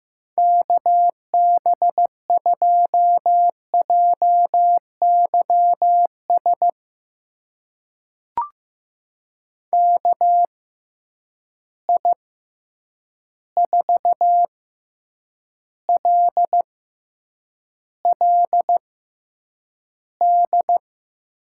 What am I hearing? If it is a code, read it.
Morse code, heard as a keyed tone, about 15 wpm, spaced as for 8 wpm, 700 Hz. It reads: KB2JYS KI4LLD
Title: USA Call Signs - 6 Characters - 6x Character Spacing - 15wpm